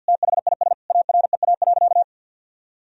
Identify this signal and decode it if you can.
Morse code, heard as a keyed tone, about 53 wpm, 700 Hz. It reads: THIS AREA.
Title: Sentences from Top 1000 Words - Review Part 2 - 50wpm